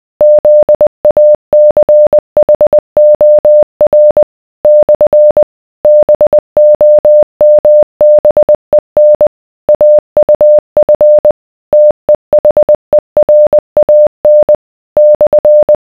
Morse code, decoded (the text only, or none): ZACHOR / BOMBEN AUF TEHERAN /